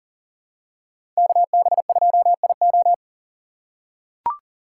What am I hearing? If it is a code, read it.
Morse code, heard as a keyed tone, about 40 wpm, 700 Hz. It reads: KB2IO
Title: USA Call Signs - 5 Characters - 4x Character Spacing - 40wpm